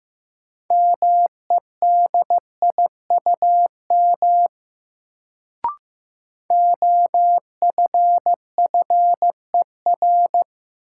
Morse code, heard as a keyed tone, about 15 wpm, 700 Hz. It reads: MEDIUM OFFER
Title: Sentences from Top 900 Words - New Word Drill - 15wpm